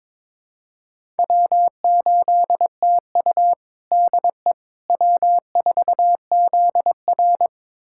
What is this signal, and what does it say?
Morse code, heard as a keyed tone, about 21 wpm, 700 Hz. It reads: W8TU DE W4ZR
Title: USA Callsign DE Callsign - 4 Characters - 22wpm